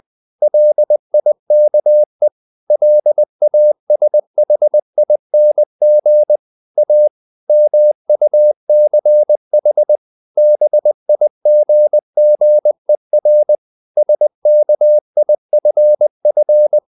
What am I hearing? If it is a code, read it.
Morse code, heard as a keyed tone, about 20 wpm, 600 Hz. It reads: LIKE LASHING A MUCH BIGGER SKIFF
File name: the old man and the sea part 2 20wpm